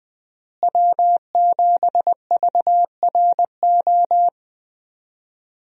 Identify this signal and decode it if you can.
Morse code, heard as a keyed tone, about 20 wpm, 700 Hz. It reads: W7VRO
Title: USA Call Signs - 5 Characters - Repeated 3x - 20wpm